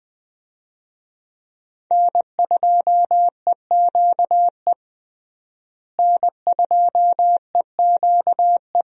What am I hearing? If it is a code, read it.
Morse code, heard as a keyed tone, about 20 wpm, 700 Hz. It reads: N2EQE N2EQE
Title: USA Call Signs - 5 Characters - Repeated 2x - 20wpm